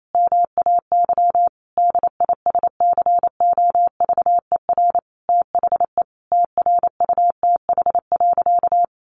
Morse code, heard as a keyed tone, about 28 wpm, 700 Hz. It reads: MAY DISCOVER THE TRUTH.
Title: Sentences from Top 1100 Words 28wpm